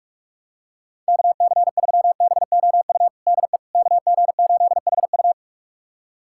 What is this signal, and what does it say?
Morse code, heard as a keyed tone, about 42 wpm, 700 Hz. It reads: KC3BOU DE KG8HU
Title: USA Callsign DE Callsign 45wpm